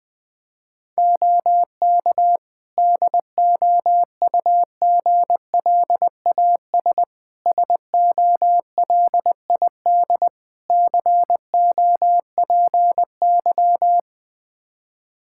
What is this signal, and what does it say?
Morse code, heard as a keyed tone, about 20 wpm, 700 Hz. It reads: OK DOUGLAS SOLID COPY
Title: Practice QSO - Exchange 3 - More Info - 20wpm